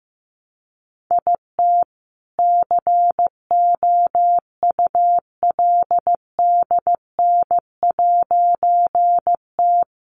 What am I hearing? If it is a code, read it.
Morse code, heard as a keyed tone, about 14 wpm, 700 Hz. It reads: IT COULDN'T
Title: A Sound Of Thunder 15wpm